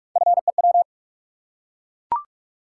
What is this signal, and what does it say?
Morse code, heard as a keyed tone, about 44 wpm, 700 Hz. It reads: FEW
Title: Three Letter Words - 45wpm